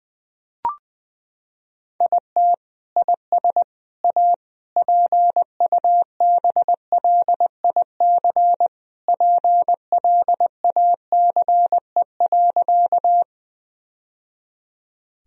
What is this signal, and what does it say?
Morse code, heard as a keyed tone, about 20 wpm, 700 Hz. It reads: IT IS A PUBLIC PLACE.
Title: Sentences from Top 1000 Words - Review Part 3 - 20wpm